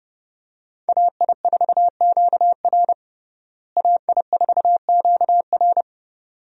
Morse code, heard as a keyed tone, about 29 wpm, 700 Hz. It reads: AI4QR AI4QR
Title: USA Call Signs - 5 Characters - Repeated 2x - 30wpm